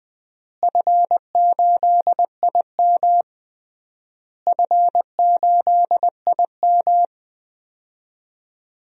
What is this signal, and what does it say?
Morse code, heard as a keyed tone, about 20 wpm, 700 Hz. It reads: F8IM F8IM
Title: DXCC-100 Call Signs - 4 Characters - Repeated 3x - 20wpm